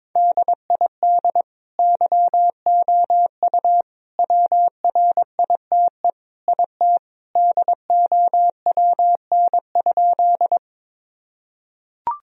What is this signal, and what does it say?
Morse code, heard as a keyed tone, about 22 wpm, 700 Hz. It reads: DID YOU WRITE IT DOWN?
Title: Sentences from Top 1000 Words - Review Part 1 - 22wpm